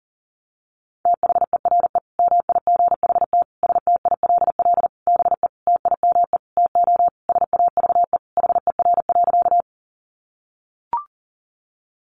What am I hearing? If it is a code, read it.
Morse code, heard as a keyed tone, about 40 wpm, 700 Hz. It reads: THERE MIGHT STILL BE TIME TO SAVE HER.
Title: Sentences from Top 1100 Words 40wpm